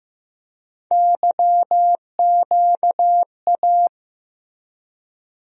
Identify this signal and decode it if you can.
Morse code, heard as a keyed tone, about 15 wpm, 700 Hz. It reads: YQA
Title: USA Call Sign Postfixes - 3 Letters 15wpm